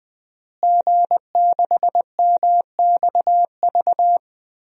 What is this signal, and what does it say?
Morse code, heard as a keyed tone, about 21 wpm, 700 Hz. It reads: G6MXV
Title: DXCC-100 Call Signs - 5 Characters - Repeated 3x - 20wpm